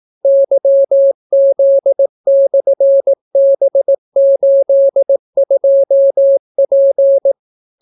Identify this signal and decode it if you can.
Morse code, heard as a keyed tone, about 18 wpm, 550 Hz. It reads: YZ/B82P